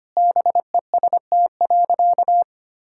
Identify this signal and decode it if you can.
Morse code, heard as a keyed tone, about 25 wpm, 700 Hz. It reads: BEST.
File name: practice / 25wpm / Sentences with 2-Word Phrases - 25wpm